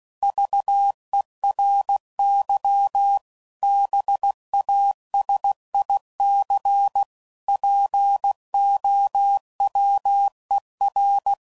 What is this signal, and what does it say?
Morse code, heard as a keyed tone, about 16 wpm, 800 Hz. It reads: VERY BASIC POWER
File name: CW Code Groups @15WPM/Technical Subjects